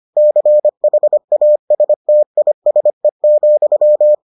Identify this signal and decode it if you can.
Morse code, heard as a keyed tone, about 25 wpm, 600 Hz. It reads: CHASTISE,